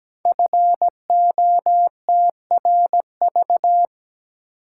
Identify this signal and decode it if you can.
Morse code, heard as a keyed tone, about 17 wpm, 700 Hz. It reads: FOTRV
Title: Code Groups - 5 Letters - 17wpm